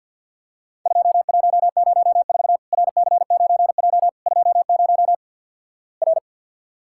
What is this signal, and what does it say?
Morse code, heard as a keyed tone, about 48 wpm, 700 Hz. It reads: 2104 RC9J 20 R
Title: Worked All Europe DX Contest - QTC Exchange with Repeats - 50wpm